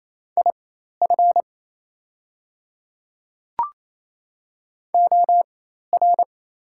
Morse code, heard as a keyed tone, about 28 wpm, spaced as for 13 wpm, 700 Hz. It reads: IF OR
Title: 4x Character Spacing - Top 100 Words - 28wpm